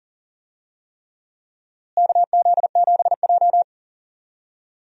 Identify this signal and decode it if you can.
Morse code, heard as a keyed tone, about 40 wpm, 700 Hz. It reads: KZ7J
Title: USA Call Signs - 4 Characters - Repeated 3x - 40wpm